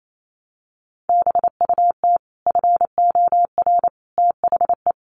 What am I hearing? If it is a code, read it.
Morse code, heard as a keyed tone, about 28 wpm, 690 Hz. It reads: BUT FOR THE
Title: Sets of 5 Words Encore 28wpm